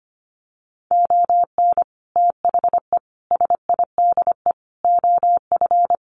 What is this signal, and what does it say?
Morse code, heard as a keyed tone, about 25 wpm, 700 Hz. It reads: ON THE SIDE OF
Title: Sets of 5 Words 25wpm